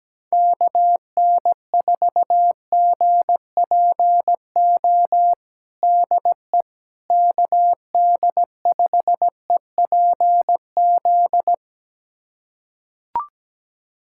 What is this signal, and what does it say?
Morse code, heard as a keyed tone, about 17 wpm, 700 Hz. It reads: KN4GPO DE KD5EPZ
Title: USA Callsign DE Callsign - 6 Characters - 17wpm